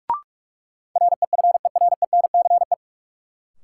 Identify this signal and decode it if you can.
Morse code, heard as a keyed tone, about 44 wpm, 700 Hz. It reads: REFERENCE
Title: Sentences from Top 1200 Words - New Word Drill - 45wpm